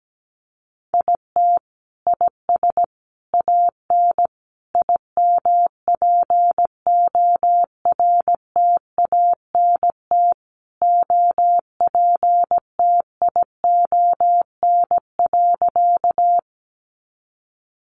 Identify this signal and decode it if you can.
Morse code, heard as a keyed tone, about 17 wpm, 700 Hz. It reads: IT IS AN IMPORTANT OPTION.
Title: Sentences from Top 1300 Words 17wpm